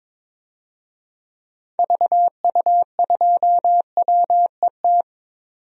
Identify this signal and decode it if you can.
Morse code, heard as a keyed tone, about 21 wpm, 700 Hz. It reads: VU2WET
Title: DXCC-100 Call Signs - 6 Characters - 22wpm